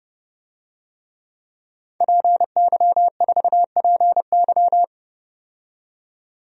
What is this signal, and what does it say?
Morse code, heard as a keyed tone, about 30 wpm, 700 Hz. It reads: PY4PY
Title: DXCC-100 Call Signs - 5 Characters - 30wpm